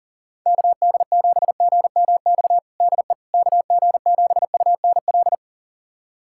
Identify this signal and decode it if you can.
Morse code, heard as a keyed tone, about 40 wpm, 690 Hz. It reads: KD7GMX DE KG7UNL
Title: USA Callsign DE Callsign - 6 Characters - 40wpm